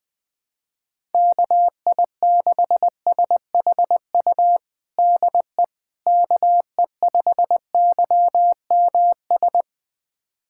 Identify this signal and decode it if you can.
Morse code, heard as a keyed tone, about 20 wpm, 700 Hz. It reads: KI6SHU DE KE5YMS
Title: USA Callsign DE Callsign - 6 Characters - 20wpm